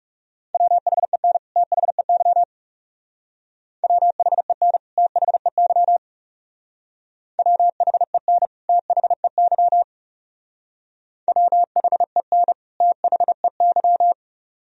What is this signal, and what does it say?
Morse code, heard as a keyed tone, about 37 wpm, 700 Hz. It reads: WHEN THEY WHEN THEY WHEN THEY WHEN THEY
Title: Speed-Racing - Sets of 2 Words 30wpm